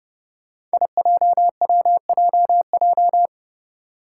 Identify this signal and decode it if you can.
Morse code, heard as a keyed tone, about 30 wpm, 700 Hz. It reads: IJWJJ
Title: Five Challenging Letter-Number Combinations 30wpm